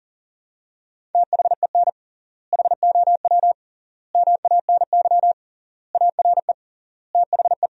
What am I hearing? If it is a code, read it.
Morse code, heard as a keyed tone, about 40 wpm, 700 Hz. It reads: THEN HOW MANY ARE THE
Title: 3x Word-Spacing - Sentences from Top 100 Words - 40wpm